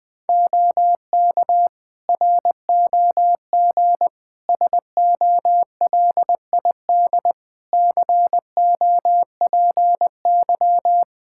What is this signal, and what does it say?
Morse code, heard as a keyed tone, about 20 wpm, 700 Hz. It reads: OK ROG SOLID COPY
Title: Practice QSO - Exchange 3 - More Info - 20wpm